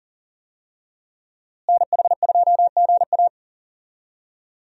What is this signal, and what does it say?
Morse code, heard as a keyed tone, about 40 wpm, 700 Hz. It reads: NH2GA
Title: USA Call Signs - 5 Characters - Repeated 3x - 40wpm